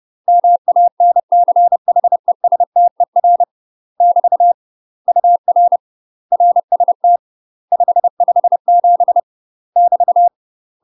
Morse code, heard as a keyed tone, about 30 wpm, 700 Hz. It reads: MANCHESTER <BT> UR RST 557 <BT>
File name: UK MASTER QSOs TRAINING / 30WPM / UK MASTER QSOs 0044